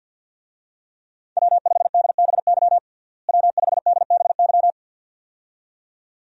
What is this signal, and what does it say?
Morse code, heard as a keyed tone, about 52 wpm, 700 Hz. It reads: W5DBY W5DBY
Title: USA Call Signs - 5 Characters - Repeated 2x - 50wpm